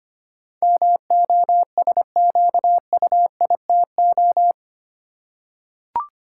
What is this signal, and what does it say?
Morse code, heard as a keyed tone, about 25 wpm, 700 Hz. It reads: MOSQUITO
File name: Eight Letter Words - 25wpm